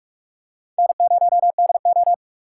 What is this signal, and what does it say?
Morse code, heard as a keyed tone, about 44 wpm, 700 Hz. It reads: N0DO